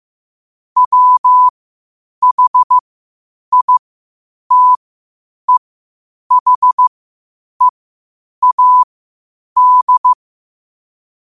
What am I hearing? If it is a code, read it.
Morse code, heard as a keyed tone, about 15 wpm, spaced as for 8 wpm, 1000 Hz. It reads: WHITEHEAD